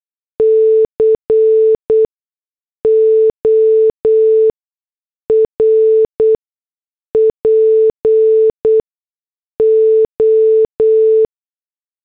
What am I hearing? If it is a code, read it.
Morse code, heard as a keyed tone, about 8 wpm, spaced as for 6 wpm, 440 Hz. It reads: CORPO
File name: morse / headlines